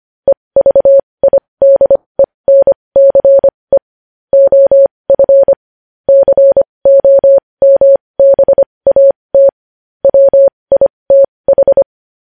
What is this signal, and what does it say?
Morse code, heard as a keyed tone, about 25 wpm, spaced as for 20 wpm, 550 Hz. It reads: EVIDENCE OF COMBAT WITH
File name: podcast / News headlines-1x-25x20